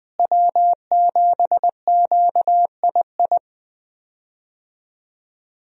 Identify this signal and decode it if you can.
Morse code, heard as a keyed tone, about 20 wpm, 700 Hz. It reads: W7QII
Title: USA Call Signs - 5 Characters - Repeated 2x - 20wpm